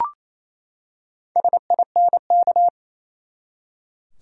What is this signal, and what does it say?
Morse code, heard as a keyed tone, about 28 wpm, 700 Hz. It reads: SINK
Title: Four Letter Words - 28wpm